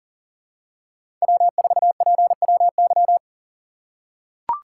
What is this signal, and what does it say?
Morse code, heard as a keyed tone, about 40 wpm, 700 Hz. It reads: W4PWY
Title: USA Call Signs - 4x Character Spacing - 40wpm